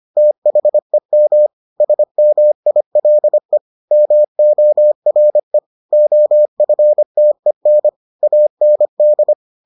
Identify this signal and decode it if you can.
Morse code, heard as a keyed tone, about 25 wpm, 600 Hz. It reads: THEM SMILE MORE OFTEN AND